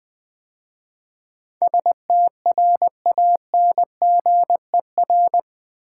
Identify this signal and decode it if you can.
Morse code, heard as a keyed tone, about 20 wpm, 700 Hz. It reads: STRANGER
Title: 8 Letter Words - First 3 Letters Given - 20wpm